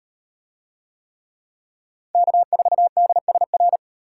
Morse code, heard as a keyed tone, about 38 wpm, 700 Hz. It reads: K4DSR